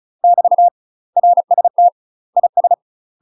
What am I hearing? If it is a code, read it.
Morse code, heard as a keyed tone, about 34 wpm, 700 Hz. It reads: <BT> RST IS
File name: UK MASTER QSOs TRAINING / 35WPM / UK MASTER QSOs 0040